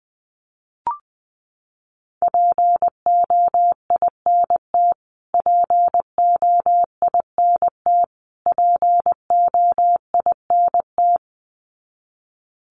Morse code, heard as a keyed tone, about 20 wpm, 700 Hz. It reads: POINT POINT POINT
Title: Top 300 Words Repeated 20wpm